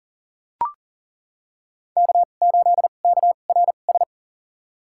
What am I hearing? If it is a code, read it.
Morse code, heard as a keyed tone, about 40 wpm, spaced as for 29 wpm, 700 Hz. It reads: K8KRS
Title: USA Call Signs - 5 Characters - 2x Character Spacing - 40wpm